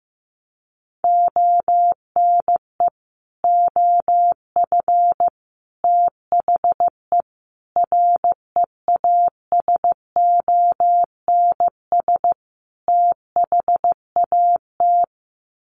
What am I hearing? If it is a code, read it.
Morse code, heard as a keyed tone, about 15 wpm, 700 Hz. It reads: ONE OF THE REASONS THAT
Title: Sets of 5 Words Encore 15wpm